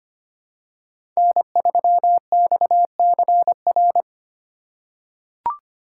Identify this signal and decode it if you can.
Morse code, heard as a keyed tone, about 25 wpm, 700 Hz. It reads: N3XCR
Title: USA Call Signs - 5 Characters - 3x Character Spacing - 25wpm